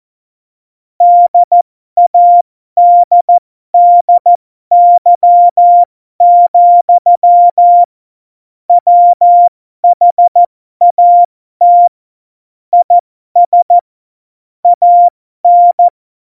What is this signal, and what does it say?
Morse code, heard as a keyed tone, about 14 wpm, spaced as for 12 wpm, 700 Hz. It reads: DADDY, WHAT IS AN
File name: Morsekurs / Witze